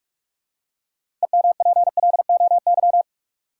Morse code, heard as a keyed tone, about 44 wpm, 700 Hz. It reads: EMPLOY
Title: Six Letter Words - 45wpm